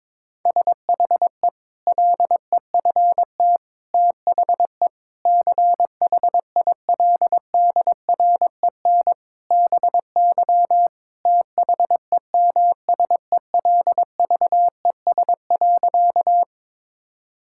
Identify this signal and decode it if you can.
Morse code, heard as a keyed tone, about 22 wpm, 700 Hz. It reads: SHE LEFT THE CHILDREN BY THEMSELVES.
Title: Sentences from Top 1000 Words - Review Part 5 - 22wpm